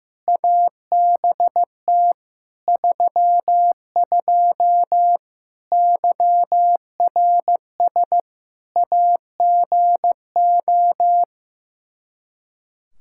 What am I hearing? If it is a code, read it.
Morse code, heard as a keyed tone, about 16 wpm, 700 Hz. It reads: ABT 32 YRS AGO
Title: Practice QSO - Exchange 3 - More Info - 15wpm